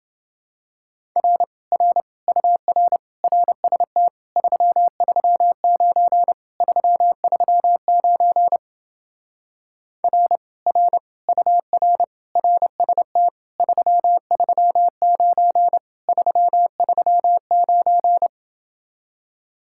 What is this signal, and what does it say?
Morse code, heard as a keyed tone, about 29 wpm, 700 Hz. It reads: R R UR RST 339 339 R R UR RST 339 339